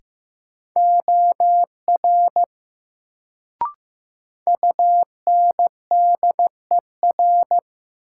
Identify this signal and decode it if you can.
Morse code, heard as a keyed tone, about 15 wpm, 700 Hz. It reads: OR UNDER
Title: Sets of 2 Words - New Word Drill - 15wpm